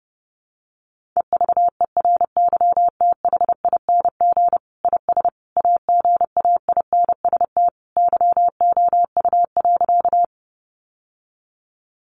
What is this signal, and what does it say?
Morse code, heard as a keyed tone, about 30 wpm, 700 Hz. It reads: EVERYTHING IS AGAINST YOU.
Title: Sentences from Top 700 Words 30wpm